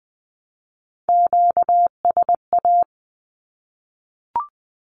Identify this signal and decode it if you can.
Morse code, heard as a keyed tone, about 20 wpm, 700 Hz. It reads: QSA
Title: Q-Codes 20wpm